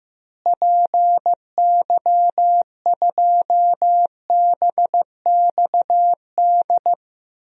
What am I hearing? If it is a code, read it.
Morse code, heard as a keyed tone, about 15 wpm, 700 Hz. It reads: PY2BXD
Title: DXCC-100 Call Signs - 6 Characters - Repeated 2x - 15wpm